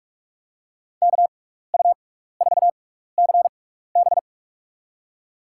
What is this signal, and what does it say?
Morse code, heard as a keyed tone, about 44 wpm, spaced as for 15 wpm, 690 Hz. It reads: KU4CB
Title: USA Call Signs - 5 Characters - 6x Character Spacing - 45wpm